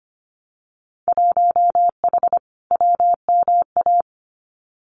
Morse code, heard as a keyed tone, about 25 wpm, 700 Hz. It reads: 1H WMA
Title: Winter Field Day Exchanges 25wpm